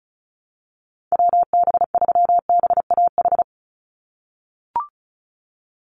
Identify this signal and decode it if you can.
Morse code, heard as a keyed tone, about 34 wpm, 700 Hz. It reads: WB3BAH
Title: USA Call Signs 35wpm